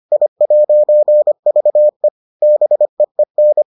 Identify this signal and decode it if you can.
Morse code, heard as a keyed tone, about 25 wpm, 600 Hz. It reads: I'VE BEEN